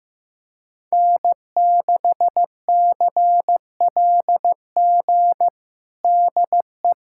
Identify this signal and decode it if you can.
Morse code, heard as a keyed tone, about 16 wpm, 700 Hz. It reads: N6CLG DE
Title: USA Callsign DE Callsign - 5 Characters - 15wpm